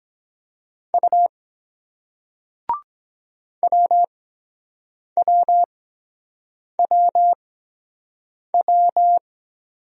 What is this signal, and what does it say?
Morse code, heard as a keyed tone, about 23 wpm, spaced as for 7 wpm, 700 Hz. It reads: U WWWW